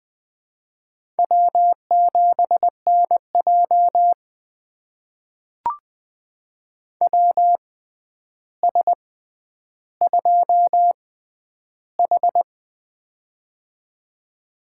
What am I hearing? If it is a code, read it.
Morse code, heard as a keyed tone, about 21 wpm, spaced as for 10 wpm, 700 Hz. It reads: W7NJ WS2H
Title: USA Call Signs - 4 Characters - 6x Character Spacing - 20wpm